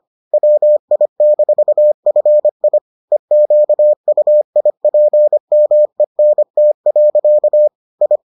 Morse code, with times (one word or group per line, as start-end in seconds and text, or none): 0.3-2.8 WI-FI
3.1-7.7 EQUIPMENT.
8.0-8.2 I